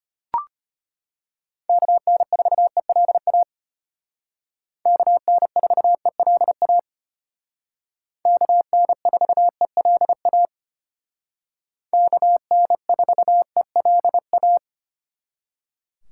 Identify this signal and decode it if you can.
Morse code, heard as a keyed tone, about 32 wpm, 700 Hz. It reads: KN4ELA KN4ELA KN4ELA KN4ELA